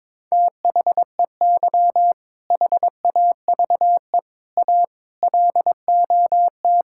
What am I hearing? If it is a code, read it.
Morse code, heard as a keyed tone, about 22 wpm, 700 Hz. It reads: THEY HAVE A LOT
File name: practice / 22wpm / Sentences with 4-Word Phrases - 22wpm